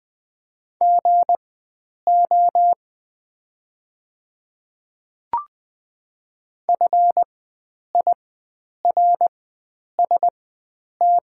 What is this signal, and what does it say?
Morse code, heard as a keyed tone, about 20 wpm, spaced as for 9 wpm, 700 Hz. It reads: GO FIRST
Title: 4x Character Spacing - Top 100 Words - 20wpm